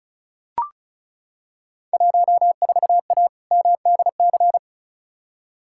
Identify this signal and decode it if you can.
Morse code, heard as a keyed tone, about 36 wpm, 700 Hz. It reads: 14A MDC
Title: Field Day Exchanges Encore - 35wpm